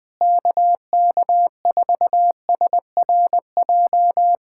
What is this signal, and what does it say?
Morse code, heard as a keyed tone, about 20 wpm, 700 Hz. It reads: KK4SRJ
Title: Speed-Racing Common QSO Phrases Encore - 20wpm